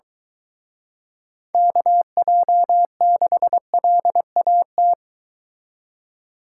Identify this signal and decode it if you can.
Morse code, heard as a keyed tone, about 22 wpm, 700 Hz. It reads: KJ6LAT